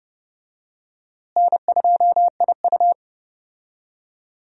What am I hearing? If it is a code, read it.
Morse code, heard as a keyed tone, about 30 wpm, 700 Hz. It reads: N2IU
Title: USA Call Signs - 4 Characters - Repeated 2x - 30wpm